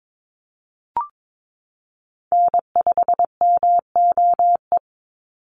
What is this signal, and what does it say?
Morse code, heard as a keyed tone, about 22 wpm, 700 Hz. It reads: N5MOE
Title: USA Call Signs 22wpm